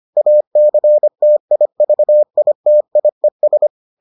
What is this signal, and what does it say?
Morse code, heard as a keyed tone, about 25 wpm, 600 Hz. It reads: ACTIVITIES